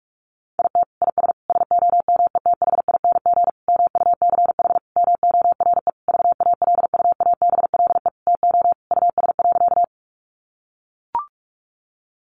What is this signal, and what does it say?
Morse code, heard as a keyed tone, about 44 wpm, 700 Hz. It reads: IT IS SOMETHING MUCH MORE VALUABLE TO US.
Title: Sentences from Top 1100 Words 45wpm